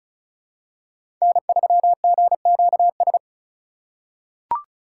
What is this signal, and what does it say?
Morse code, heard as a keyed tone, about 34 wpm, 700 Hz. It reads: N3GQS
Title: USA Call Signs - 5 Characters - 3x Character Spacing - 35wpm